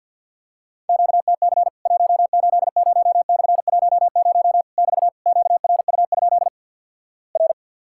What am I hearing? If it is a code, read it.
Morse code, heard as a keyed tone, about 50 wpm, 700 Hz. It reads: QTC 180/10 <BT> QRU? R
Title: Worked All Europe DX Contest - QTC Exchange - 50wpm